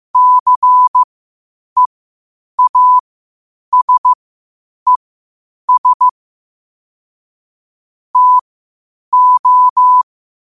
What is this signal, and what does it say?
Morse code, heard as a keyed tone, about 15 wpm, spaced as for 8 wpm, 1000 Hz. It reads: CEASES TO
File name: qotdfiles / 8wpm